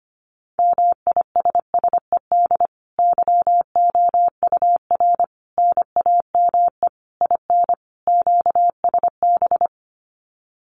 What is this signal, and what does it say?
Morse code, heard as a keyed tone, about 25 wpm, 700 Hz. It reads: MISSED YOUR NAME IN QSB
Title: Common QSO Phrases 25wpm